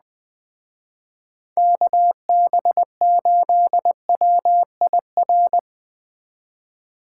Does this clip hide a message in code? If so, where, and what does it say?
Yes: Morse code from 1.6 to 5.6 s, heard as a keyed tone, about 20 wpm, 700 Hz. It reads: KB8WIR